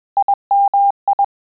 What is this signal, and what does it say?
Morse code, heard as a keyed tone, about 21 wpm, 800 Hz. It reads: IMI
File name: QSO CW 100 CPM